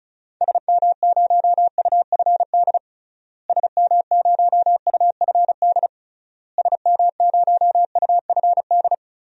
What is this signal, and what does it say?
Morse code, heard as a keyed tone, about 36 wpm, 700 Hz. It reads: SM0UFD SM0UFD SM0UFD
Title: DXCC-100 Call Signs - 6 Characters - Repeated 3x - 35wpm